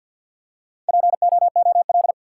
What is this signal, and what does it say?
Morse code, heard as a keyed tone, about 49 wpm, 700 Hz. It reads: POOL
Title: Four Letter Words - 50wpm